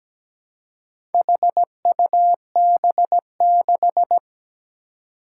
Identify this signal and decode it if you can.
Morse code, heard as a keyed tone, about 17 wpm, 700 Hz. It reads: HUB6
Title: Four Challenging Letter-Number Combinations 17wpm